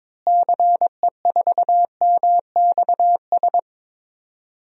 Morse code, heard as a keyed tone, about 22 wpm, 700 Hz. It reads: CE4MXS
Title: DXCC-100 Call Signs - 6 Characters - Repeated 3x - 22wpm